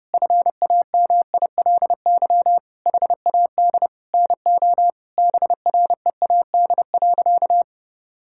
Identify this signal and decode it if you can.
Morse code, heard as a keyed tone, about 30 wpm, 700 Hz. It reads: FAMILY HAD NO BREAD.
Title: Sentences from Top 1600 Words 30wpm